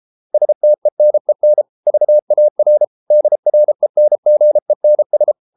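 Morse code, heard as a keyed tone, about 33 wpm, 600 Hz. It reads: STENEN VAR DRENGENS
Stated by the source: Morse code at 33 wpm, QRQ.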